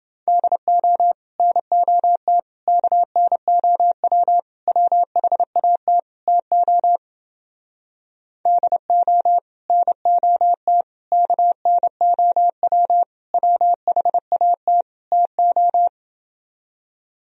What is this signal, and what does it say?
Morse code, heard as a keyed tone, about 31 wpm, 700 Hz. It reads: DO NOT KNOW WHAT TO DO NOT KNOW WHAT TO